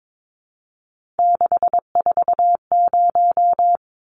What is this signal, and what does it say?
Morse code, heard as a keyed tone, about 22 wpm, 700 Hz. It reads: 640
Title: Numbers - 3-Digit 22wpm